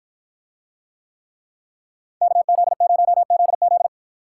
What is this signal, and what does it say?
Morse code, heard as a keyed tone, about 49 wpm, 700 Hz. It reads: KZ0ZZ